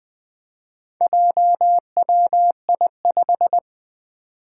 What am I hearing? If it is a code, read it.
Morse code, heard as a keyed tone, about 20 wpm, 700 Hz. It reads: JWI5
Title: Four Challenging Letter-Number Combinations 20wpm